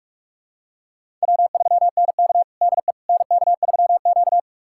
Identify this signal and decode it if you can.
Morse code, heard as a keyed tone, about 46 wpm, 700 Hz. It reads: W3NK DE NK3Q
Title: USA Callsign DE Callsign - 4 Characters - 45wpm